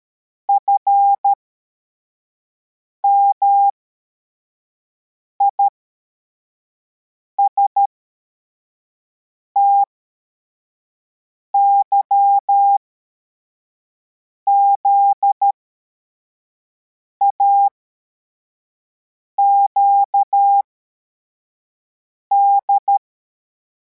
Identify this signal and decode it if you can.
Morse code, heard as a keyed tone, about 13 wpm, spaced as for 4 wpm, 800 Hz. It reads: FMIST YZAQD
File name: formation / cw / Text73